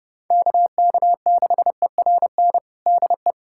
Morse code, heard as a keyed tone, about 32 wpm, 700 Hz. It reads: KK6ERN DE
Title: USA Callsign DE Callsign - 6 Characters - 30wpm